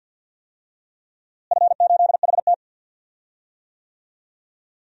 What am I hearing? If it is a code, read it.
Morse code, heard as a keyed tone, about 51 wpm, 700 Hz. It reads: F8HT
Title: DXCC-100 Call Signs - 4 Characters - Repeated 3x - 50wpm